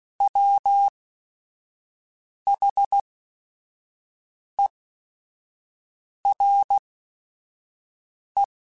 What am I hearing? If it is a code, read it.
Morse code, heard as a keyed tone, about 16 wpm, spaced as for 5 wpm, 800 Hz. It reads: WHERE
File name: Chuckles / Chuckles-mp3s / MiddleAge @5